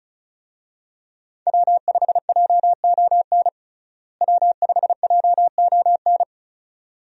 Morse code, heard as a keyed tone, about 36 wpm, 700 Hz. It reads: W5JON W5JON
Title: USA Call Signs - 5 Characters - Repeated 2x - 35wpm